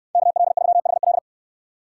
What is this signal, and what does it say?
Morse code, heard as a keyed tone, about 69 wpm, 700 Hz. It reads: KL3HL